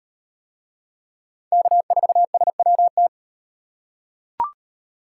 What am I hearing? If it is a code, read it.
Morse code, heard as a keyed tone, about 38 wpm, 700 Hz. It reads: K4SWT